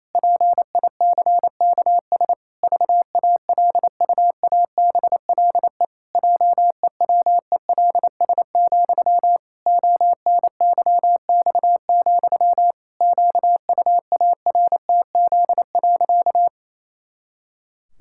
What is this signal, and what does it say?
Morse code, heard as a keyed tone, about 28 wpm, 700 Hz. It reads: PICKS VALUABLE JEWELS, ONYX, QUARTZ.
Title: Pangram Challenge v2 - 28wpm